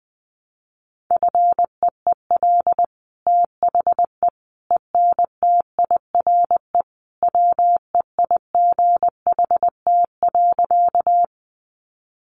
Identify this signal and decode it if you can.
Morse code, heard as a keyed tone, about 20 wpm, 700 Hz. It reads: FEEL THE ENTIRE WEIGHT.
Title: Sentences from Top 800 Words 20wpm